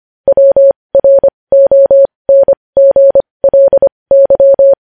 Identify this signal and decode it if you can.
Morse code, heard as a keyed tone, about 25 wpm, spaced as for 20 wpm, 550 Hz. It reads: WRONGLY
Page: News Headlines in Morse Code at 20 WPM